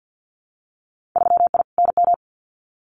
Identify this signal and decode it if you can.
Morse code, heard as a keyed tone, about 50 wpm, 700 Hz. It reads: 3I NM
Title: Winter Field Day Exchanges Encore 50wpm